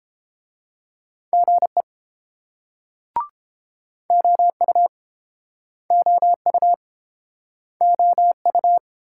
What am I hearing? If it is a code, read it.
Morse code, heard as a keyed tone, about 28 wpm, 700 Hz. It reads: GE OU OU OU